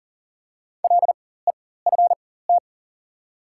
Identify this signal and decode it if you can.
Morse code, heard as a keyed tone, about 40 wpm, spaced as for 19 wpm, 700 Hz. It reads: LEFT
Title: 4x Character Spacing - Top 300 Words - 40wpm